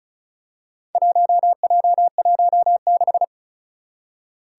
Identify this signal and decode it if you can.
Morse code, heard as a keyed tone, about 35 wpm, 690 Hz. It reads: 1J16
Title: Four Challenging Letter-Number Combinations 35wpm